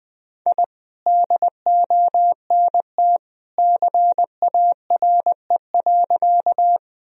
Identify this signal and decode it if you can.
Morse code, heard as a keyed tone, about 19 wpm, 700 Hz. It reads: I DONT CARE.
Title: Sentences from Top 1000 Words - Review Part 1 - 20wpm